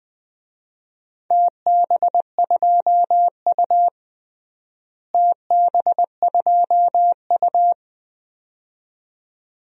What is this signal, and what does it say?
Morse code, heard as a keyed tone, about 21 wpm, 700 Hz. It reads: TB2U TB2U
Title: DXCC-100 Call Signs - 4 Characters - Repeated 2x - 20wpm